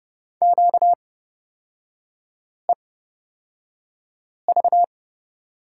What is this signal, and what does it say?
Morse code, heard as a keyed tone, about 30 wpm, spaced as for 5 wpm, 700 Hz. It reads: QEV
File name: Lesson 38e - Letter q - Three Characters 30wpm